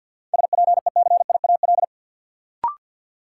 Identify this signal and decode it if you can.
Morse code, heard as a keyed tone, about 53 wpm, 700 Hz. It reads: SPECIAL
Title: Sentences from Top 1000 Words - Review Part 3 - New Word Drill - 50wpm